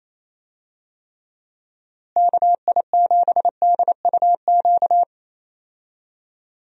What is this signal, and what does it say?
Morse code, heard as a keyed tone, about 28 wpm, 700 Hz. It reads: KI7DUQ
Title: USA Call Signs - 6 Characters - Repeated 3x - 28wpm